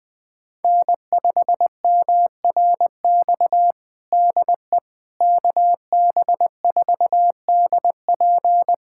Morse code, heard as a keyed tone, about 20 wpm, 700 Hz. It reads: N5MRX DE KB4DP
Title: USA Callsign DE Callsign - 5 Characters - 20wpm